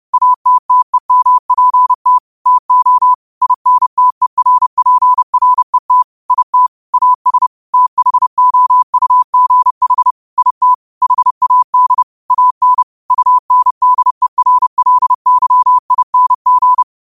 Morse code, heard as a keyed tone, about 30 wpm, 1000 Hz. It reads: ATTEMPT TO INTERPRET IT AS THOUGH IT HAD AN UNDERLYING
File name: qotdfiles / 30wpm